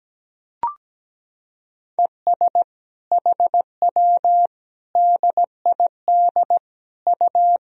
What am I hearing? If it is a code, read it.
Morse code, heard as a keyed tone, about 17 wpm, 700 Hz. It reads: ES HW DID U
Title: Sentences from Top 100 Words and Abbreviations 17wpm